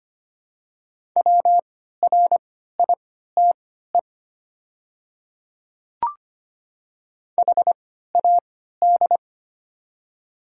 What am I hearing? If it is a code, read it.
Morse code, heard as a keyed tone, about 25 wpm, spaced as for 14 wpm, 690 Hz. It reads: WRITE HAD
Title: 3x Character Spacing - Top 200 Words - 25wpm